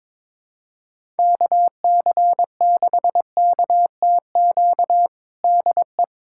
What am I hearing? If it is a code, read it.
Morse code, heard as a keyed tone, about 21 wpm, 700 Hz. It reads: KC6KTQ DE
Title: USA Callsign DE Callsign 22wpm